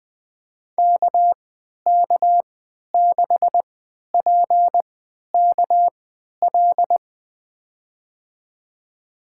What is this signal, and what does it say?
Morse code, heard as a keyed tone, about 21 wpm, spaced as for 12 wpm, 700 Hz. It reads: KK6PKL
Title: USA Call Signs - 3x Character Spacing - 20wpm